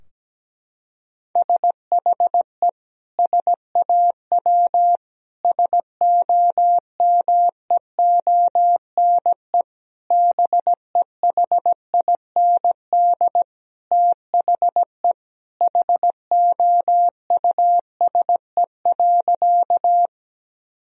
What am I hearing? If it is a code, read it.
Morse code, heard as a keyed tone, about 17 wpm, 700 Hz. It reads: SHE SAW SOMEONE BEHIND THE HOUSE.
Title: Sentences from Top 1000 Words - Review Part 4 - 17wpm